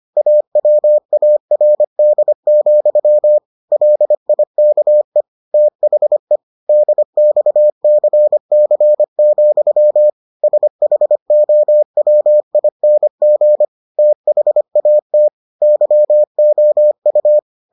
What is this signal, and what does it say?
Morse code, heard as a keyed tone, about 25 wpm, 600 Hz. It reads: AWARD, LIKE THE DXCC, SHOWING THAT YOU